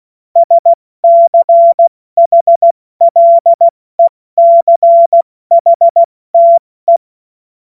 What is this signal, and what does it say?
Morse code, heard as a keyed tone, about 16 wpm, 680 Hz. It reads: SCHLECHTE
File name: Morsekurs / Witze / Witz 13. Juni 2023 70 F80